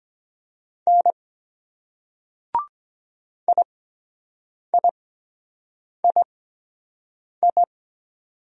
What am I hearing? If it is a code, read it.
Morse code, heard as a keyed tone, about 22 wpm, spaced as for 7 wpm, 700 Hz. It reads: N IIII